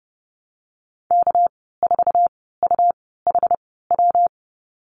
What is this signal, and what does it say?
Morse code, heard as a keyed tone, about 30 wpm, spaced as for 17 wpm, 700 Hz. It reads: K4UHW
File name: Lesson 37g - Letter x - Call Signs 30wpm